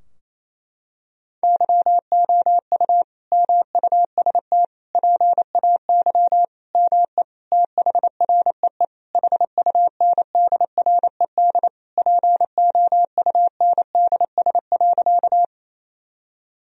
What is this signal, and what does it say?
Morse code, heard as a keyed tone, about 28 wpm, 700 Hz. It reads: YOU MUST PAY ME THREE HUNDRED POUNDS.
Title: Sentences from Top 1000 Words - Review Part 2 - 28wpm